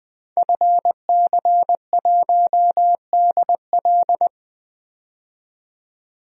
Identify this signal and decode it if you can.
Morse code, heard as a keyed tone, about 21 wpm, 700 Hz. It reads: FC1DL
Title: DXCC-100 Call Signs - 5 Characters - 20wpm